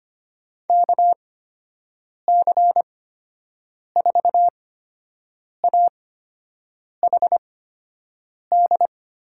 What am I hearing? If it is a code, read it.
Morse code, heard as a keyed tone, about 25 wpm, spaced as for 7 wpm, 700 Hz. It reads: KC4AHD
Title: USA Call Signs - 6 Characters - 8x Character Spacing - 25wpm